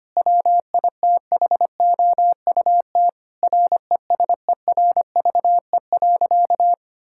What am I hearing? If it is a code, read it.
Morse code, heard as a keyed tone, about 25 wpm, 700 Hz. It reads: WITHOUT RESERVE.
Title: Sentences from Top 1000 Words - Review Part 6 - 25wpm